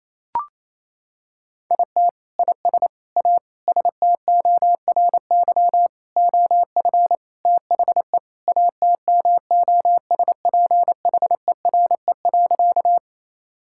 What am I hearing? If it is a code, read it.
Morse code, heard as a keyed tone, about 28 wpm, 700 Hz. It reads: IT IS A STORY OF THE ATMOSPHERE.
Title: Sentences from Top 1600 Words 28wpm